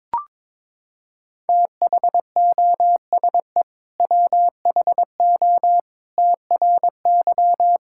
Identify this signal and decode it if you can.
Morse code, heard as a keyed tone, about 22 wpm, 700 Hz. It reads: THOSE WHO TRY
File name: practice / 22wpm / Sentences with 2-Word Phrases - 22wpm